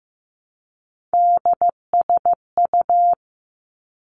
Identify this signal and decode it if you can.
Morse code, heard as a keyed tone, about 15 wpm, 700 Hz. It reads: DSU
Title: Three Challenging Letter-Number Combinations 15wpm